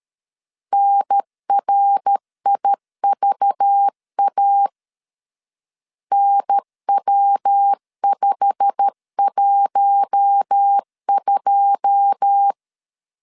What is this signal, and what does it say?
Morse code, heard as a keyed tone, about 13 wpm, 800 Hz. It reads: NRIVA NW512
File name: LEZIONI / 9 2 12wpm 2301